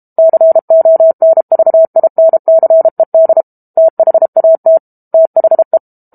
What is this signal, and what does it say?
Morse code, heard as a keyed tone, about 32 wpm, 650 Hz. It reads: CONVINCED THAT THE
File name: Book Two / 06-Chapter 6 - 32 WPM